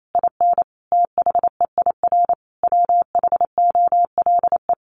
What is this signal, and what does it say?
Morse code, heard as a keyed tone, about 28 wpm, 700 Hz. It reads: IN THEIR WHOLE
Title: Sentences from Top 1500 Words 28wpm